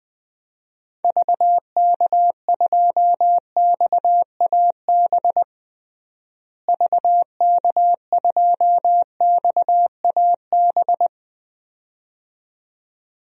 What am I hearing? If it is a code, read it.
Morse code, heard as a keyed tone, about 21 wpm, 700 Hz. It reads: VK2XAB VK2XAB
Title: DXCC-100 Call Signs - 6 Characters - Repeated 2x - 20wpm